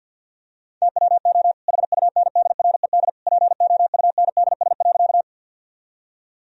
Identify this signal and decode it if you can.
Morse code, heard as a keyed tone, about 50 wpm, 700 Hz. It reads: TWO HUNDRED POUNDS.
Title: Sentences from Top 1000 Words - Review Part 2 - 50wpm